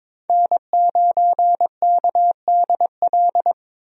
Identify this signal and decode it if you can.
Morse code, heard as a keyed tone, about 22 wpm, 700 Hz. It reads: N9KDL
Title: USA Call Signs - 5 Characters - 22wpm